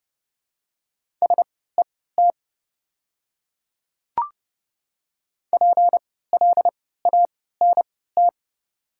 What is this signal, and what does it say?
Morse code, heard as a keyed tone, about 30 wpm, spaced as for 17 wpm, 700 Hz. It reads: SET PLANT
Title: 3x Character Spacing - Top 500 Words - 30wpm